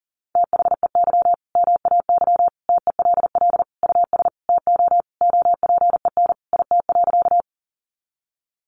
Morse code, heard as a keyed tone, about 40 wpm, 700 Hz. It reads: THEY MAY TELL US TO OPEN IT.
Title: Sentences from Top 300 Words 40wpm